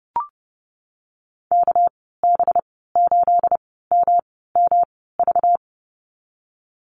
Morse code, heard as a keyed tone, about 30 wpm, spaced as for 17 wpm, 700 Hz. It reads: KB8MMV
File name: Lesson 37g - Letter x - Call Signs 30wpm